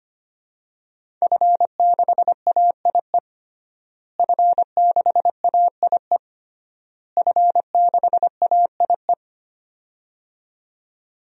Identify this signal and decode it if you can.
Morse code, heard as a keyed tone, about 25 wpm, 700 Hz. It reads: F6AIE F6AIE F6AIE
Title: DXCC-100 Call Signs - 5 Characters - Repeated 3x - 25wpm